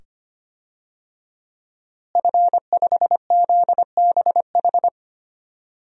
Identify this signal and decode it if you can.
Morse code, heard as a keyed tone, about 25 wpm, 700 Hz. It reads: F5ZBH
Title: DXCC-100 Call Signs - 5 Characters - Repeated 2x - 25wpm